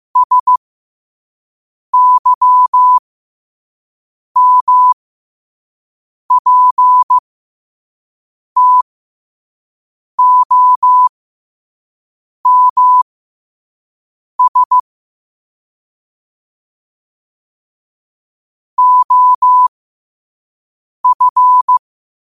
Morse code, heard as a keyed tone, about 15 wpm, spaced as for 5 wpm, 1000 Hz. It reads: SYMPTOMS OF
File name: qotdfiles / 5wpm